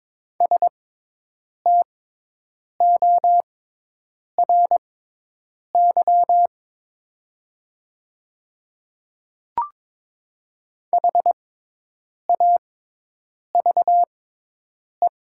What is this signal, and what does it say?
Morse code, heard as a keyed tone, about 22 wpm, spaced as for 8 wpm, 690 Hz. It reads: STORY HAVE
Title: 6x Character Spacing - Top 300 Words - 22wpm